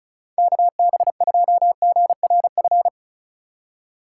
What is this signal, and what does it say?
Morse code, heard as a keyed tone, about 36 wpm, 700 Hz. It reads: KB2GRF
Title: USA Call Signs - 6 Characters - 3x Character Spacing - 35wpm